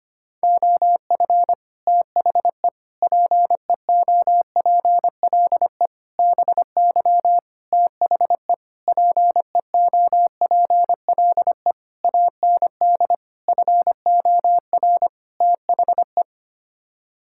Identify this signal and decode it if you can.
Morse code, heard as a keyed tone, about 25 wpm, 700 Hz. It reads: OF THE PEOPLE BY THE PEOPLE AND FOR THE
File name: Sets of 9 Words - 25wpm